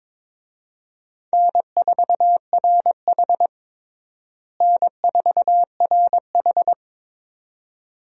Morse code, heard as a keyed tone, about 22 wpm, 700 Hz. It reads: N4RH N4RH
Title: USA Call Signs - 4 Characters - Repeated 2x - 22wpm